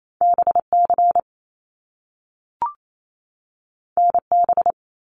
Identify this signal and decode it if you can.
Morse code, heard as a keyed tone, about 28 wpm, 700 Hz. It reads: BC NB
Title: Canadian Provinces and Territories 28wpm